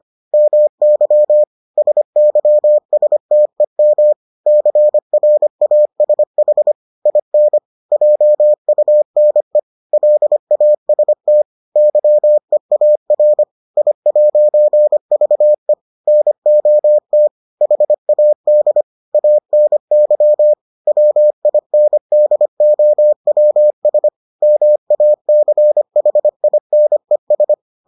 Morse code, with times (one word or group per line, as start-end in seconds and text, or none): 0.3-1.4 MY
1.8-4.1 SYSTEM
4.5-6.7 CRASH
7.1-7.6 IN
7.9-9.6 JUNE
9.9-11.4 LAST
11.8-13.4 YEAR
13.8-15.7 I'VE
16.1-17.3 NOT
17.6-18.8 HAD
19.1-20.5 ANY
20.9-24.1 WINDOWS
24.4-27.5 MACHINES